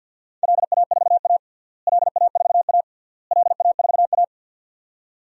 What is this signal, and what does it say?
Morse code, heard as a keyed tone, about 51 wpm, 700 Hz. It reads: LA4A LA4A LA4A
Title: DXCC-100 Call Signs - 4 Characters - Repeated 3x - 50wpm